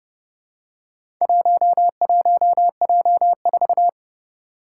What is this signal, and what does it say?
Morse code, heard as a keyed tone, about 29 wpm, 700 Hz. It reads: 11J4
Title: Four Challenging Letter-Number Combinations 30wpm